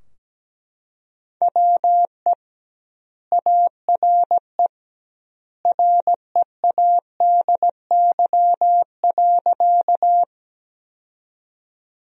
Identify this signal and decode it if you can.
Morse code, heard as a keyed tone, about 17 wpm, 700 Hz. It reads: WE ARE READY.
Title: 2x Word-Spacing - Sentences from Top 400 Words - 17wpm